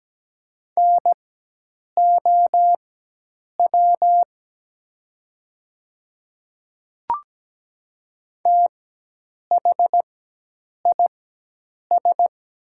Morse code, heard as a keyed tone, about 17 wpm, spaced as for 8 wpm, 690 Hz. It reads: NOW THIS
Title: 4x Character Spacing - Top 200 Words - 17wpm